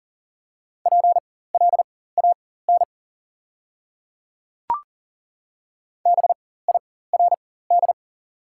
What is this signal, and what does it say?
Morse code, heard as a keyed tone, about 40 wpm, spaced as for 19 wpm, 690 Hz. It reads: PLAN BIRD
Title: 4x Character Spacing - Top 500 Words - 40wpm